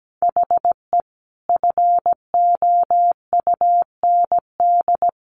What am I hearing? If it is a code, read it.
Morse code, heard as a keyed tone, about 17 wpm, 700 Hz. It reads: HE FOUND
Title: Sentences from Top 1400 Words 17wpm